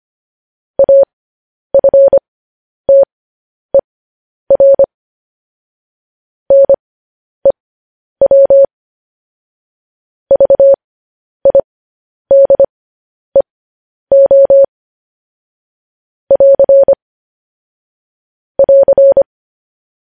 Morse code, heard as a keyed tone, about 25 wpm, spaced as for 10 wpm, 550 Hz. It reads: AFTER NEW VIDEO <AR> <AR>
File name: podcast / News headlines-1x-25x10